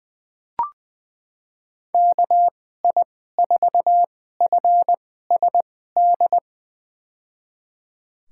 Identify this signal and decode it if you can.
Morse code, heard as a keyed tone, about 21 wpm, spaced as for 15 wpm, 700 Hz. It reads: KI4FSD
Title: USA Call Signs - 6 Characters - 2x Character Spacing - 20wpm